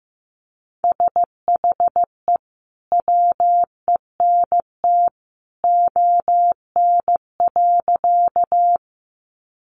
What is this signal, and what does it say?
Morse code, heard as a keyed tone, about 15 wpm, 700 Hz. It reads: SHE WENT ON.
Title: Sentences from Top 200 Words 15wpm